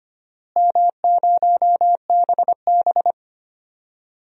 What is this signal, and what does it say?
Morse code, heard as a keyed tone, about 25 wpm, 700 Hz. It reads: M0BB
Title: DXCC-100 Call Signs - 4 Characters - Repeated 3x - 25wpm